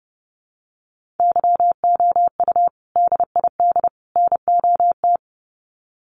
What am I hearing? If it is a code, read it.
Morse code, heard as a keyed tone, about 30 wpm, 700 Hz. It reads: YOU DID NOT
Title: Sets of 3 Words 30wpm